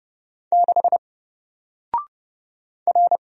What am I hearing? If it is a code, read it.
Morse code, heard as a keyed tone, about 30 wpm, 700 Hz. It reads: 6 R
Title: Single Letter-Number Alt 30wpm